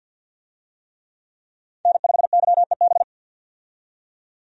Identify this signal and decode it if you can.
Morse code, heard as a keyed tone, about 47 wpm, 700 Hz. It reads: N5YEB
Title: USA Call Signs - 5 Characters - 50wpm